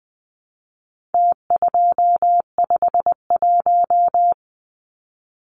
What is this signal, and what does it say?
Morse code, heard as a keyed tone, about 20 wpm, 700 Hz. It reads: T251
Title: Numbers - Contest Style - 4-Digit 20wpm